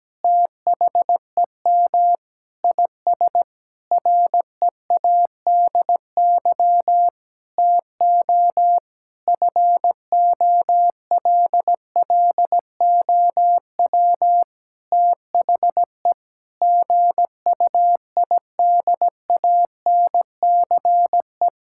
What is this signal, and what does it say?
Morse code, heard as a keyed tone, about 17 wpm, 700 Hz. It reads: THEM IS READY TO FOLLOW THE GUIDANCE
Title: Sentences from Top 1600 Words 17wpm